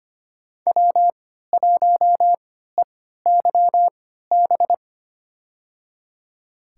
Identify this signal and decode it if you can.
Morse code, heard as a keyed tone, about 25 wpm, spaced as for 14 wpm, 700 Hz. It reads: W1EYB
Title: USA Call Signs - 5 Characters - 3x Character Spacing - 25wpm